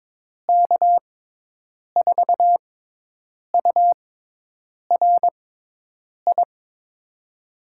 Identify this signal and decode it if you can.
Morse code, heard as a keyed tone, about 22 wpm, spaced as for 8 wpm, 700 Hz. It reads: K4URI
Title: USA Call Signs - 6 Characters - 6x Character Spacing - 22wpm